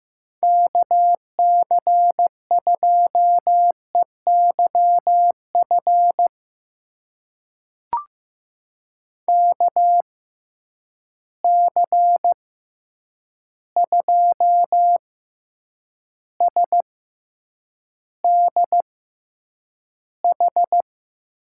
Morse code, heard as a keyed tone, about 15 wpm, spaced as for 8 wpm, 700 Hz. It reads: KC2EYF KC2SDH
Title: USA Call Signs - 6 Characters - 6x Character Spacing - 15wpm